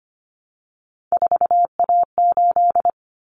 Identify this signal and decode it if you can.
Morse code, heard as a keyed tone, about 25 wpm, 700 Hz. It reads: 4A8
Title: Common Cut Numbers - 3-Digit 25wpm